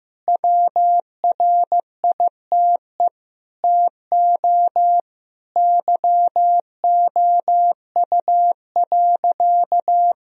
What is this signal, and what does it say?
Morse code, heard as a keyed tone, about 15 wpm, 700 Hz. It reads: WRITE TO YOU.
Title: Sentences from Top 1000 Words - Review Part 1 - 15wpm